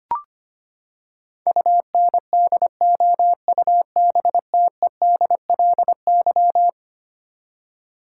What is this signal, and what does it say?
Morse code, heard as a keyed tone, about 25 wpm, 700 Hz. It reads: UNDOUBTEDLY
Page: morse-code-ninja-practice-25wpm